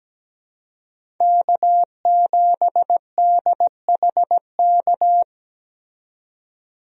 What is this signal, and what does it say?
Morse code, heard as a keyed tone, about 17 wpm, 700 Hz. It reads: K7DHK
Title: USA Call Signs - 5 Characters - 17wpm